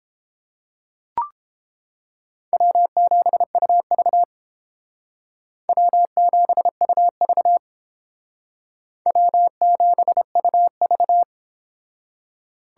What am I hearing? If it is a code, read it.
Morse code, heard as a keyed tone, about 31 wpm, 700 Hz. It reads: W7UV W7UV W7UV